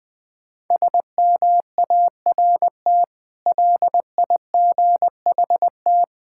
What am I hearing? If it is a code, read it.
Morse code, heard as a keyed tone, about 21 wpm, 700 Hz. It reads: SMART LIGHT
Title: Difficult Words with Hints - 20wpm